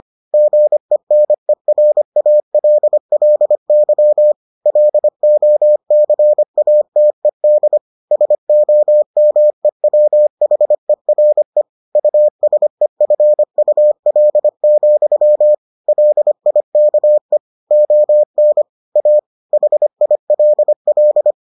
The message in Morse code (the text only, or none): GENERALLY LOCATED SOMEWHERE USEFUL, LIKE ON A HILL